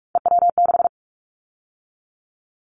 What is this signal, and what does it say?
Morse code, heard as a keyed tone, about 44 wpm, 700 Hz. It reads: EW6
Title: Callsign Prefixes for Top 100 DXCC - 45wpm